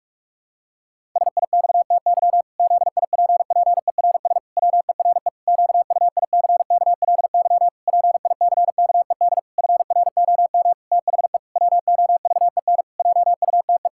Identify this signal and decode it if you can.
Morse code, heard as a keyed tone, about 41 wpm, 700 Hz. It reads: SIXTY ZIPPERS WERE QUICKLY PICKED FROM THE WOVEN JUTE
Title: Pangram Challenge v2 - 45wpm